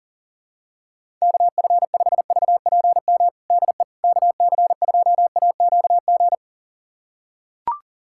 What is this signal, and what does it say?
Morse code, heard as a keyed tone, about 37 wpm, 700 Hz. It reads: KF5VPM DE KC2AQG
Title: USA Callsign DE Callsign - 6 Characters - 40wpm